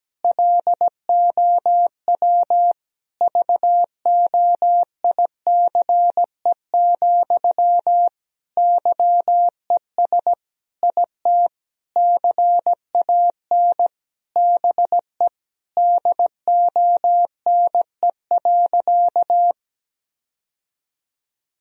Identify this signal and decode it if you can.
Morse code, heard as a keyed tone, about 17 wpm, 700 Hz. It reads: LOW VOICE, YES IT CAN BE DONE.
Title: Sentences from Top 1000 Words - Review Part 1 - 17wpm